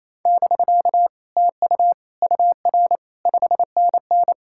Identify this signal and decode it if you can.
Morse code, heard as a keyed tone, about 37 wpm, 700 Hz. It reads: <BK> TU UR 5NN